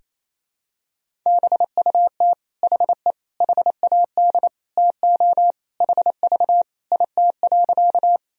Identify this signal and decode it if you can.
Morse code, heard as a keyed tone, about 28 wpm, 700 Hz. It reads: BUT HE HAD TO HV IT.
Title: Sentences from Top 100 Words and Abbreviations 28wpm